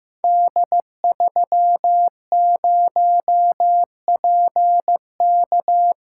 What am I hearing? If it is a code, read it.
Morse code, heard as a keyed tone, about 15 wpm, 700 Hz. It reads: D30PK
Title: Code Groups - 5 Letter:Numbers - 15wpm